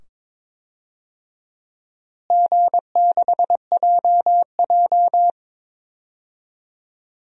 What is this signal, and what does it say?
Morse code, heard as a keyed tone, about 23 wpm, 700 Hz. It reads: G6JJ
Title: DXCC-100 Call Signs - 4 Characters - Repeated 2x - 22wpm